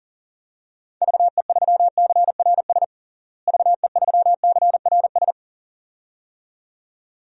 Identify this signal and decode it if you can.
Morse code, heard as a keyed tone, about 40 wpm, 700 Hz. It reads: VE3CRS VE3CRS
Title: DXCC-100 Call Signs - 6 Characters - Repeated 2x - 40wpm